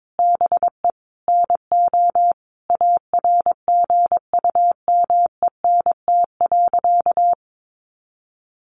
Morse code, heard as a keyed tone, about 22 wpm, 700 Hz. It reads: BE NO ARGUMENT.
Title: A Sound Of Thunder 22wpm